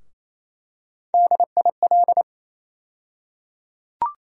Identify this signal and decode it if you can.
Morse code, heard as a keyed tone, about 28 wpm, 700 Hz. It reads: DIL
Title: USA Call Sign Postfixes - 3 Letters 28wpm